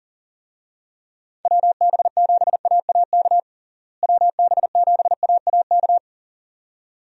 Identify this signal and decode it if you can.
Morse code, heard as a keyed tone, about 40 wpm, 700 Hz. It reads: WB7AAK WB7AAK
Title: USA Call Signs - 6 Characters - Repeated 2x - 40wpm